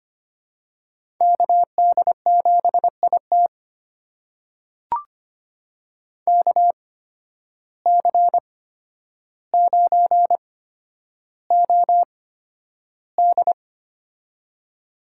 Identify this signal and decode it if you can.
Morse code, heard as a keyed tone, about 25 wpm, spaced as for 11 wpm, 700 Hz. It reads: KD7IT KC9OD
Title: USA Call Signs - 5 Characters - 8x Character Spacing - 25wpm